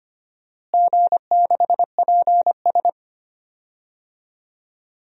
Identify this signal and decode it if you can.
Morse code, heard as a keyed tone, about 25 wpm, 700 Hz. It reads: G6PS
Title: DXCC-100 Call Signs - 4 Characters - Repeated 3x - 25wpm